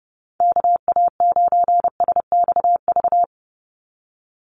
Morse code, heard as a keyed tone, about 30 wpm, 700 Hz. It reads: KA9SXV
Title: Call Signs - 30wpm